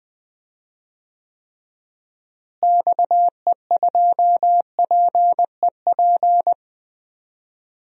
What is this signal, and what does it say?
Morse code, heard as a keyed tone, about 20 wpm, 700 Hz. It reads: XE2PEP
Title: DXCC-100 Call Signs - 6 Characters - Repeated 2x - 20wpm